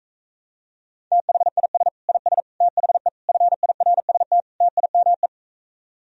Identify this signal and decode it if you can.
Morse code, heard as a keyed tone, about 45 wpm, 700 Hz. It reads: THIS IS THE FIRST TIME